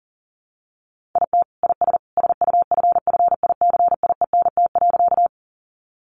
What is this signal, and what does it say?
Morse code, heard as a keyed tone, about 40 wpm, 700 Hz. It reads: IT IS SUFFICIENT.
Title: Sentences from Top 1400 Words 40wpm